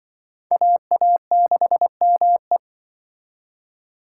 Morse code, heard as a keyed tone, about 24 wpm, 700 Hz. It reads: AA6ME
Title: Speed Racing - USA 2x2 Call Signs 20wpm